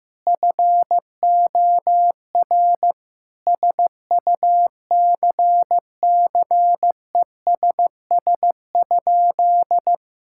Morse code, heard as a keyed tone, about 15 wpm, 700 Hz. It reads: FOR SUCCESS?
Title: Sentences from Top 1000 Words - Review Part 5 - 15wpm